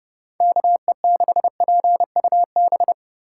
Morse code, heard as a keyed tone, about 38 wpm, 700 Hz. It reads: KE6PUB